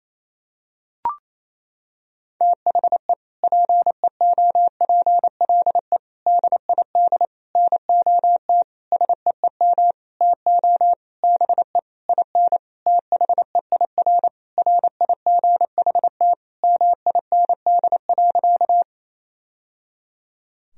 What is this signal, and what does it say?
Morse code, heard as a keyed tone, about 28 wpm, 700 Hz. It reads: THE PEOPLE DID NOT SEEM TO BE IN THEIR RIGHT MIND.
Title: Sentences from Top 1000 Words - Review Part 1 - 28wpm